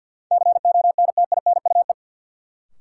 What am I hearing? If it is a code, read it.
Morse code, heard as a keyed tone, about 52 wpm, 700 Hz. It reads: CONTINUE
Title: Sentences from Top 1100 Words - New Word Drill Alt - 50wpm